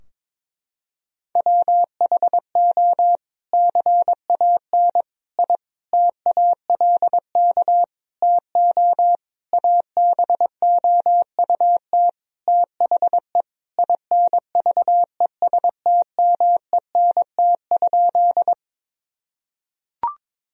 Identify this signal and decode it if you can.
Morse code, heard as a keyed tone, about 22 wpm, 700 Hz. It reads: WHO CAN I TALK TO ABOUT THE INVESTMENT?
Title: Sentences from Top 1000 Words - Review Part 4 - 22wpm